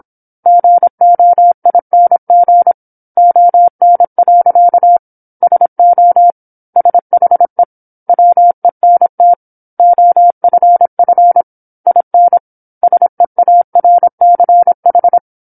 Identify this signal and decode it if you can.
Morse code, heard as a keyed tone, about 27 wpm, 700 Hz. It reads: GOING ON. SO SHE WENT OFF IN SEARCH
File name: 08-Chapter-08 26 WPM